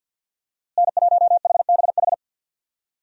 Morse code, heard as a keyed tone, about 52 wpm, 700 Hz. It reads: N1HBH
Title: USA Call Signs - 5 Characters - Repeated 2x - 50wpm